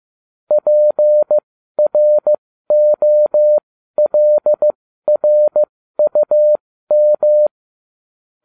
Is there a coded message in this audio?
Yes: Morse code from 0.5 to 7.5 s, heard as a keyed tone, about 15 wpm, spaced as for 12 wpm, 600 Hz. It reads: PROLRUM